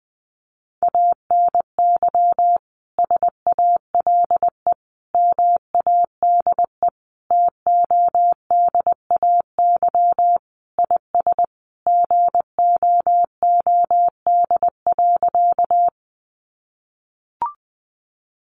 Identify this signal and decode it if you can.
Morse code, heard as a keyed tone, about 19 wpm, 700 Hz. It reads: ANY SALE MADE TODAY IS GOOD.
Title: Sentences from Top 1400 Words 20wpm